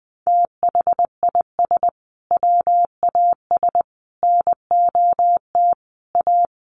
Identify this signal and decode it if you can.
Morse code, heard as a keyed tone, about 20 wpm, 700 Hz. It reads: THIS WAS NOT A
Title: Sentences from Top 1500 Words 20wpm